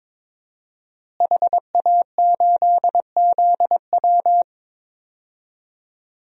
Morse code, heard as a keyed tone, about 21 wpm, 700 Hz. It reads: HA8ZW
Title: DXCC-100 Call Signs - 5 Characters - Repeated 2x - 22wpm